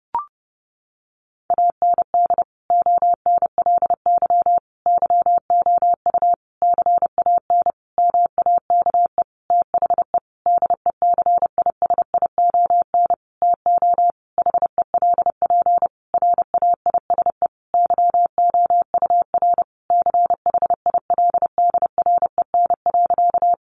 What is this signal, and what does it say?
Morse code, heard as a keyed tone, about 30 wpm, 700 Hz. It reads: AND ONLY YOU CAN MAKE THE DECISION TO HELP RAISE YOUR CHILDREN.
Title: Sentences from Top 1400 Words 30wpm